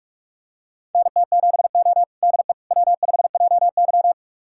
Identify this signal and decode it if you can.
Morse code, heard as a keyed tone, about 42 wpm, 700 Hz. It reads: NT7O DE W5JY
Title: USA Callsign DE Callsign - 4 Characters - 45wpm